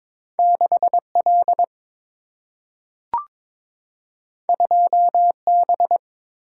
Morse code, heard as a keyed tone, about 22 wpm, 700 Hz. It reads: 6 L 2 B
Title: Single-Letter Single-Number 22wpm